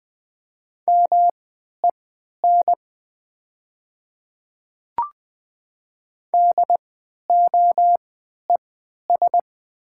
Morse code, heard as a keyed tone, about 20 wpm, spaced as for 11 wpm, 700 Hz. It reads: MEN DOES
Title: 3x Character Spacing - Top 200 Words - 20wpm